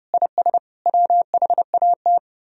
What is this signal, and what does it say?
Morse code, heard as a keyed tone, about 32 wpm, 700 Hz. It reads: IS WHAT